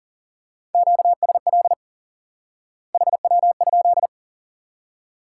Morse code, heard as a keyed tone, about 40 wpm, 700 Hz. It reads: QSL HW?
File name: IQE Course 3 - Lesson 89 - YAGI - Two Elements - 40wpm